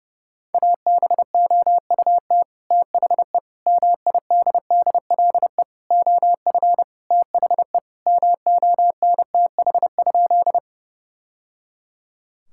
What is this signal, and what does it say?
Morse code, heard as a keyed tone, about 32 wpm, 700 Hz. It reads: ABOUT THE MIDDLE OF THE MONTH?
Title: Sentences from Top 1000 Words - Review Part 6 - 30wpm